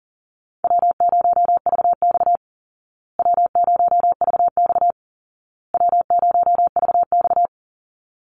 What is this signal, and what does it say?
Morse code, heard as a keyed tone, about 40 wpm, 700 Hz. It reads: W0VX W0VX W0VX
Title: Easy Call Signs Repeated 3x - 40wpm